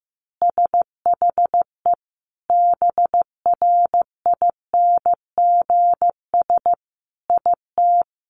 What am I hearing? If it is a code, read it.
Morse code, heard as a keyed tone, about 15 wpm, 700 Hz. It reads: SHE BRINGS IT
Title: Easy Phrases 15wpm